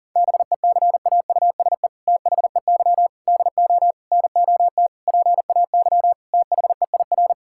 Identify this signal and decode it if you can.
Morse code, heard as a keyed tone, about 37 wpm, 700 Hz. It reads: BECAUSE THEY DO NOT PAY THEIR
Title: Sentences from Top 1000 Words - Review Part 5 - 40wpm